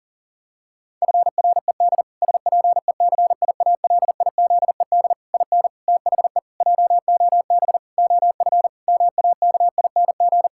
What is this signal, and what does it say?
Morse code, heard as a keyed tone, about 37 wpm, 700 Hz. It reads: FRED SPECIALIZED IN THE JOB OF MAKING
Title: Pangram Challenge v2 - 40wpm